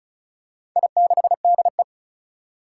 Morse code, heard as a keyed tone, about 36 wpm, 700 Hz. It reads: I6DE
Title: Four Challenging Letter-Number Combinations 35wpm